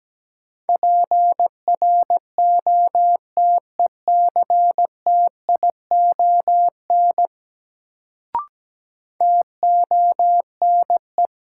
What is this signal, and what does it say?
Morse code, heard as a keyed tone, about 17 wpm, 700 Hz. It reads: PROTECTION TONE
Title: Sentences from Top 1600 Words - New Word Drill Alt - 17wpm